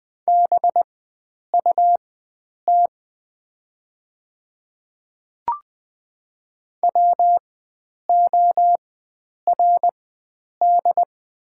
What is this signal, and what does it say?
Morse code, heard as a keyed tone, about 20 wpm, spaced as for 9 wpm, 690 Hz. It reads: BUT WORD
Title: 4x Character Spacing - Top 100 Words - 20wpm